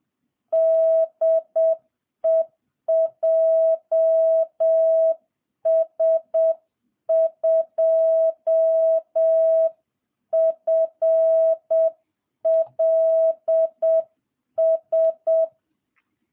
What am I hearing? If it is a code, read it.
Morse code, heard as a keyed tone, about 7 wpm, 660 Hz. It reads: DE JS2FLS